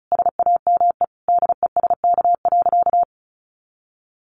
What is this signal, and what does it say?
Morse code, heard as a keyed tone, about 34 wpm, 700 Hz. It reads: SAME DESK.
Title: A Sound Of Thunder 35wpm